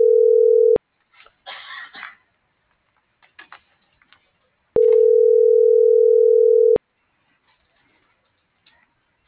Ringback tone (North America), 440 + 480 Hz.
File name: CENTRAL SOTOVA / out carlos.sotova - .77867